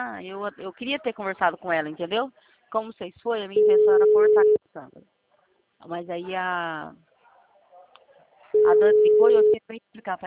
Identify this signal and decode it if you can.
Ringback tone (Europe, 425 Hz), with speech.